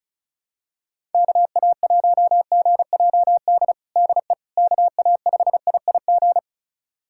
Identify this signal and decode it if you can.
Morse code, heard as a keyed tone, about 36 wpm, 690 Hz. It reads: KA1GJD DE KA5IIG
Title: USA Callsign DE Callsign - 6 Characters - 35wpm